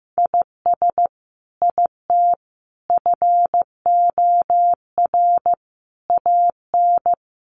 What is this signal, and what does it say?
Morse code, heard as a keyed tone, about 14 wpm, 700 Hz. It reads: IS IT FOR AN
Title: Sentences from Top 1200 Words 15wpm